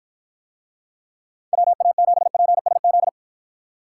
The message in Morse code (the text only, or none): WA7PSZ